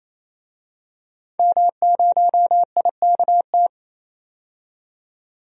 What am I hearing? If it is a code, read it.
Morse code, heard as a keyed tone, about 28 wpm, 700 Hz. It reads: M0IKT
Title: DXCC-100 Call Signs - 5 Characters - 28wpm